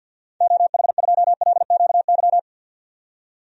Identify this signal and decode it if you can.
Morse code, heard as a keyed tone, about 51 wpm, 700 Hz. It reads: OH2LQY
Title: DXCC-100 Call Signs - 6 Characters - 50wpm